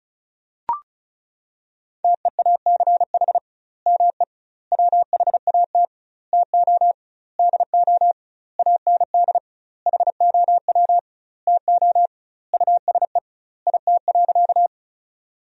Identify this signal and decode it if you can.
Morse code, heard as a keyed tone, about 36 wpm, 700 Hz. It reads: TEACH ME WHAT TO DO AND HOW TO USE IT.
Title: 2x Word-Spacing - Sentences from Top 500 Words - 35wpm